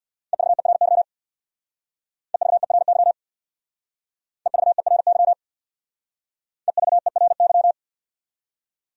Morse code, heard as a keyed tone, about 65 wpm, 690 Hz. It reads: EVERY EVERY EVERY EVERY